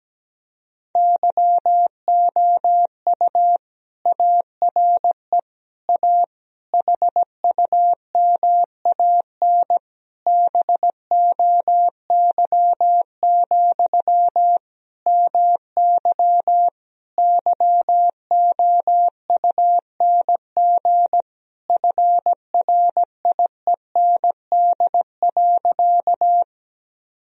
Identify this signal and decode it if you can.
Morse code, heard as a keyed tone, about 17 wpm, 700 Hz. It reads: YOU ARE A HUMAN BOY, MY YOUNG FRIEND.
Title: Sentences from Top 1000 Words - Review Part 3 - 17wpm